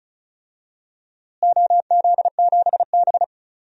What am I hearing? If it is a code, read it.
Morse code, heard as a keyed tone, about 36 wpm, 700 Hz. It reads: OZ7B
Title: DXCC-100 Call Signs - 4 Characters - Repeated 3x - 35wpm